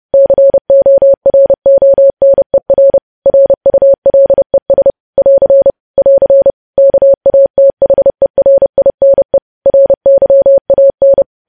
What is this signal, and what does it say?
Morse code, heard as a keyed tone, about 30 wpm, 550 Hz. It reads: CORONER RULES <AR> <AR> KATHERINE RYAN
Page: news headlines in morse code at 30 wpm